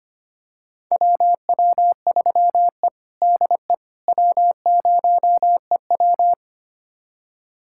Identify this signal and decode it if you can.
Morse code, heard as a keyed tone, about 25 wpm, 700 Hz. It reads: WW3E DE W0EW
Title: USA Callsign DE Callsign - 4 Characters - 25wpm